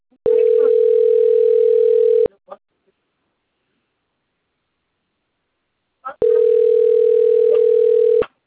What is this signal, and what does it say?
Ringback tone (North America), 440 + 480 Hz.